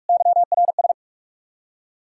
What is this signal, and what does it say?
Morse code, heard as a keyed tone, about 64 wpm, spaced as for 54 wpm, 700 Hz. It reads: YRS
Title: Speed-Racing Common QSO Phrases - 45wpm